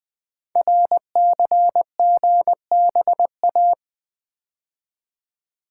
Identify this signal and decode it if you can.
Morse code, heard as a keyed tone, about 21 wpm, 700 Hz. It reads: RCGBA
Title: Code Groups - 5 Letters - 20wpm